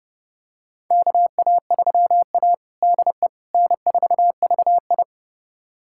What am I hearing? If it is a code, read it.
Morse code, heard as a keyed tone, about 30 wpm, 700 Hz. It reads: KA3A DE N4VI
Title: USA Callsign DE Callsign - 4 Characters - 30wpm